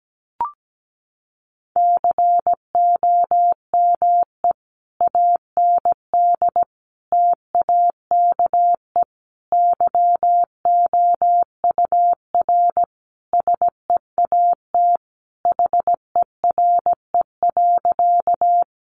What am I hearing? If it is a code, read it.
Morse code, heard as a keyed tone, about 17 wpm, 700 Hz. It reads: COME AND TAKE YOUR SEAT HERE.
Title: Sentences from Top 1500 Words 17wpm